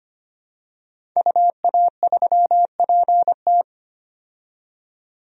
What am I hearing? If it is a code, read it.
Morse code, heard as a keyed tone, about 25 wpm, 700 Hz. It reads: UA3PT
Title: DXCC-100 Call Signs - 5 Characters - 25wpm